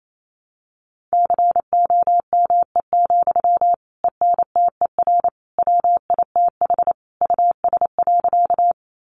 Morse code, heard as a keyed tone, about 28 wpm, 700 Hz. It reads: COME, ENTER WITH US.
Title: Sentences from Top 1300 Words 28wpm